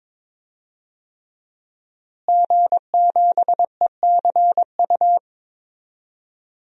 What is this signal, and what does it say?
Morse code, heard as a keyed tone, about 22 wpm, 700 Hz. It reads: G7ECU
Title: DXCC-100 Call Signs - 5 Characters - Repeated 3x - 22wpm